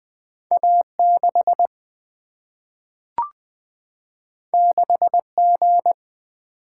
Morse code, heard as a keyed tone, about 20 wpm, 700 Hz. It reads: A 6 6 G
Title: Single-Letter Single-Number 20wpm